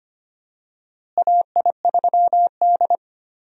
Morse code, heard as a keyed tone, about 25 wpm, 700 Hz. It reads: AI3D
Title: USA Call Signs - 4 Characters - Repeated 3x - 25wpm